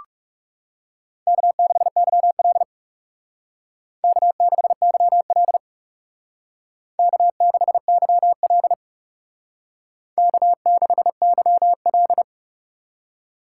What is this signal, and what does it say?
Morse code, heard as a keyed tone, about 38 wpm, 700 Hz. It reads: K6YL K6YL K6YL K6YL